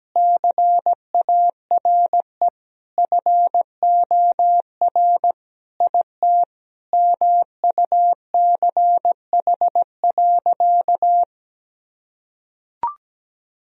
Morse code, heard as a keyed tone, about 17 wpm, 700 Hz. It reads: CARE FOR IT MUCH.
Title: Sentences from Top 1000 Words - Review Part 1 - 17wpm